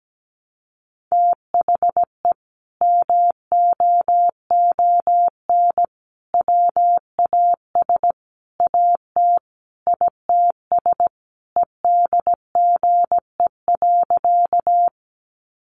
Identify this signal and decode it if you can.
Morse code, heard as a keyed tone, about 17 wpm, 700 Hz. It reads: THE MOON WAS AT ITS EDGE.
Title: Sentences from Top 1300 Words 17wpm